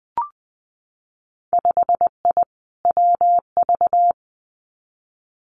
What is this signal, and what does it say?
Morse code, heard as a keyed tone, about 20 wpm, 700 Hz. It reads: 5I WV
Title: Winter Field Day Exchanges Encore 20wpm